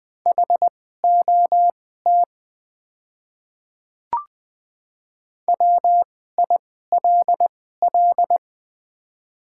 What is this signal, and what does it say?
Morse code, heard as a keyed tone, about 20 wpm, spaced as for 14 wpm, 700 Hz. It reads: HOT WILL
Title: 2x Character Spacing - Top 100 Words - 20wpm